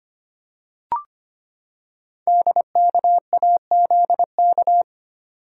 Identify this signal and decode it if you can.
Morse code, heard as a keyed tone, about 25 wpm, 700 Hz. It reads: DKAZK
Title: Code Groups - 5 Letters - 25wpm